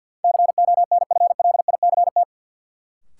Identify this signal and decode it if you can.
Morse code, heard as a keyed tone, about 53 wpm, 700 Hz. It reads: CONFLICT
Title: Sentences from Top 1300 Words - New Word Drill Alt - 50wpm